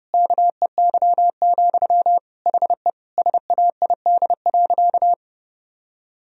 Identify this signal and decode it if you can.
Morse code, heard as a keyed tone, about 32 wpm, 700 Hz. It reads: KEY, HE SAID.
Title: Sentences from Top 1000 Words - Review Part 4 - 30wpm